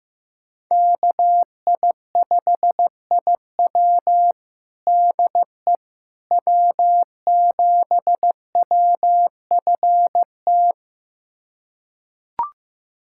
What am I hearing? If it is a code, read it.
Morse code, heard as a keyed tone, about 15 wpm, 700 Hz. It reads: KI5IW DE W7WFT
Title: USA Callsign DE Callsign 15wpm